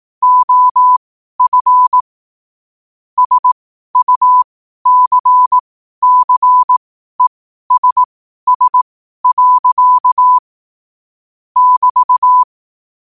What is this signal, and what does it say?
Morse code, heard as a keyed tone, about 18 wpm, spaced as for 12 wpm, 1000 Hz. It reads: OF SUCCESS. <BT>
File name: qotdfiles / 12wpm